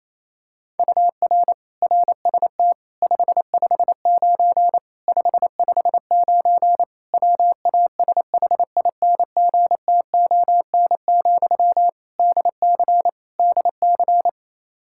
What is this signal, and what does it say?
Morse code, heard as a keyed tone, about 28 wpm, 700 Hz. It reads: UR RST 559 559 WASHINGTON, DC DC
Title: Practice QSO - Exchange 1 - Beginning - 28wpm